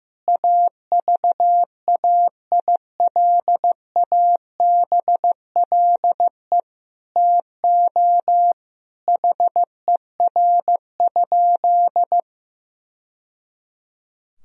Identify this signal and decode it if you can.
Morse code, heard as a keyed tone, about 15 wpm, 700 Hz. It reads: AVAILABLE TO HER?
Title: Sentences from Top 1000 Words - Review Part 3 - 15wpm